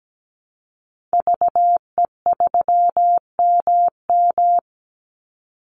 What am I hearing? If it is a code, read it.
Morse code, heard as a keyed tone, about 17 wpm, 700 Hz. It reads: VE3MM
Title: Call Signs Repeated 2x - 17wpm